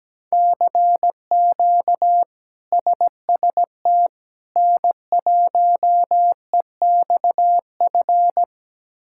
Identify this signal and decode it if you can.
Morse code, heard as a keyed tone, about 17 wpm, 700 Hz. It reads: CQ SST N1EXF
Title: K1USN SST QSO Practice - 17wpm